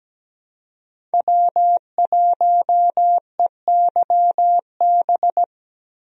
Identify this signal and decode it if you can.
Morse code, heard as a keyed tone, about 17 wpm, 700 Hz. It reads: W1EYB
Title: USA Call Signs - 5 Characters - 17wpm